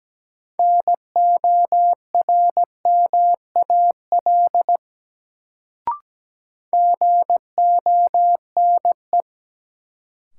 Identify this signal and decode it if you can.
Morse code, heard as a keyed tone, about 17 wpm, 700 Hz. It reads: NORMAL GONE
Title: Sentences from Top 1200 Words - New Word Drill Alt - 17wpm